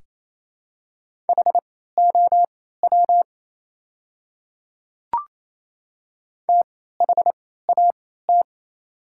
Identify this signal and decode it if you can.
Morse code, heard as a keyed tone, about 28 wpm, spaced as for 16 wpm, 690 Hz. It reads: HOW THAT
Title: 3x Character Spacing - Top 100 Words - 28wpm